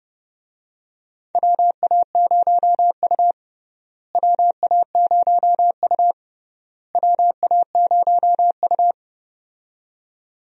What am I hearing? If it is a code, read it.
Morse code, heard as a keyed tone, about 30 wpm, 700 Hz. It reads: WA0U WA0U WA0U
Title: USA Call Signs - 4 Characters - Repeated 3x - 30wpm